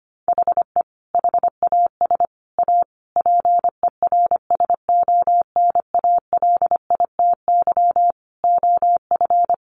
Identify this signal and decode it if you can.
Morse code, heard as a keyed tone, about 25 wpm, 700 Hz. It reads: HE HAS A PERSONALITY OF
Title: Sentences from Top 1400 Words 25wpm